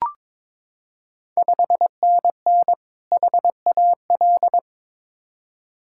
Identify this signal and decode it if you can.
Morse code, heard as a keyed tone, about 22 wpm, 700 Hz. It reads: 5NN HAL
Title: Ontario QSO Party Exchange 22wpm